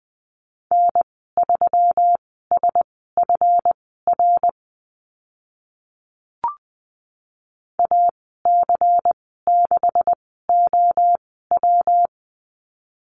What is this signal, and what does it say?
Morse code, heard as a keyed tone, about 20 wpm, spaced as for 15 wpm, 700 Hz. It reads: N3SFR AC6OW
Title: 147 Lesson 24g - Letter y - Call Signs 20wpm